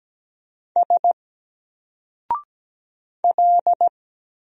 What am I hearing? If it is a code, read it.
Morse code, heard as a keyed tone, about 16 wpm, 700 Hz. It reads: S L
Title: Single Letter-Number 17wpm